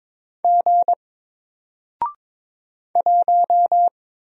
Morse code, heard as a keyed tone, about 22 wpm, 700 Hz. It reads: G 1
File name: Single-Letter-Number-22wpm